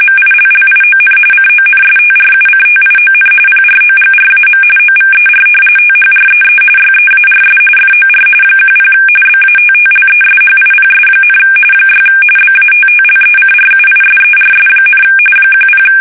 RTTY, about 45.45 baud. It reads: MILLION CONSTRUCTION PROGRAM OVER THREE YEARS FOR 10 BATTLESHIPS, 6 BATTLECRUISERS, 10
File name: fsk850